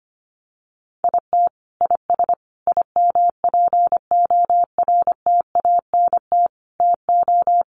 Morse code, heard as a keyed tone, about 25 wpm, 700 Hz. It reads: IT IS IMPORTANT TO
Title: Sets of 4 Words 25wpm